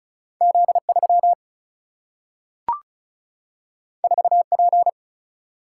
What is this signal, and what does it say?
Morse code, heard as a keyed tone, about 36 wpm, 700 Hz. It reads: Z 3 4 P
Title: Single-Letter Single-Number 35wpm